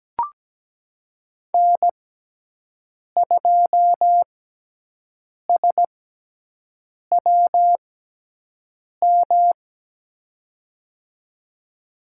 Morse code, heard as a keyed tone, about 17 wpm, spaced as for 6 wpm, 700 Hz. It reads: N2SWM
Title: USA Call Signs - 6x Character Spacing - 17wpm